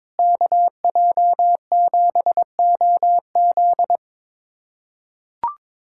Morse code, heard as a keyed tone, about 22 wpm, 700 Hz. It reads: KJ7OZ
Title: USA Callsign DE Callsign 22wpm